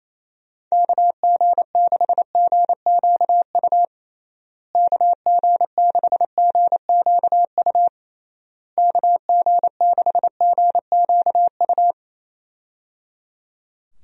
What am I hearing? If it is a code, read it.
Morse code, heard as a keyed tone, about 28 wpm, 700 Hz. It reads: KG6GQU KG6GQU KG6GQU
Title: USA Call Signs - 6 Characters - Repeated 3x - 28wpm